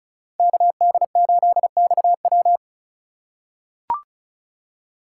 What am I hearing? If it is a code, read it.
Morse code, heard as a keyed tone, about 36 wpm, 700 Hz. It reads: KD8XW
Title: USA Call Signs - 5 Characters - 3x Character Spacing - 35wpm